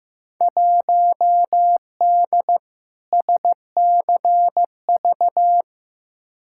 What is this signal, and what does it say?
Morse code, heard as a keyed tone, about 15 wpm, 700 Hz. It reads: 1D SCV
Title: Field Day Exchanges Encore - 15wpm